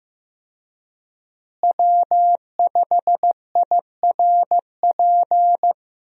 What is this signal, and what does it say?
Morse code, heard as a keyed tone, about 15 wpm, 700 Hz. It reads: W5IRP
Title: USA Call Signs - 5 Characters - Repeated 3x - 15wpm